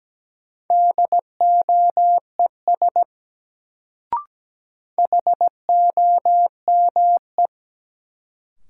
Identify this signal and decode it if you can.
Morse code, heard as a keyed tone, about 17 wpm, 700 Hz. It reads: DOES HOME
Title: Sentences from Top 200 Words - New Word Drill - 17wpm